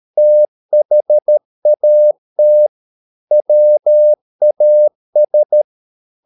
Morse code, heard as a keyed tone, about 13 wpm, 600 Hz. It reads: THAT WAS
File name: Never 0013wpm